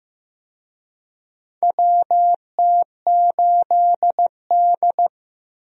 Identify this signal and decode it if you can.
Morse code, heard as a keyed tone, about 15 wpm, 700 Hz. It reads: WT8D
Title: USA Call Signs - 4 Characters - 15wpm